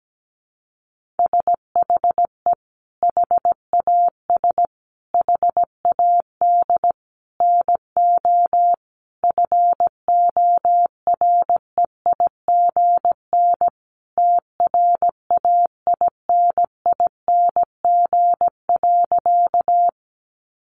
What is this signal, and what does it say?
Morse code, heard as a keyed tone, about 17 wpm, 700 Hz. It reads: SHE HAS HAD NO FOREIGN TRAINING.
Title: Sentences from Top 1400 Words 17wpm